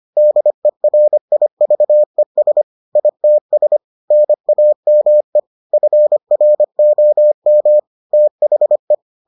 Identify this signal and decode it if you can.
Morse code, heard as a keyed tone, about 25 wpm, 600 Hz. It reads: DERIVES ITS NAME FROM THE